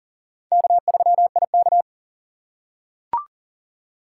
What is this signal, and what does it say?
Morse code, heard as a keyed tone, about 39 wpm, 700 Hz. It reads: K3IK
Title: USA Call Signs - 4 Characters - 2x Character Spacing - 40wpm